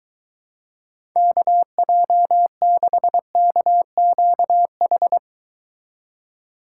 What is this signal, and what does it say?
Morse code, heard as a keyed tone, about 23 wpm, 700 Hz. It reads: KJ6KQH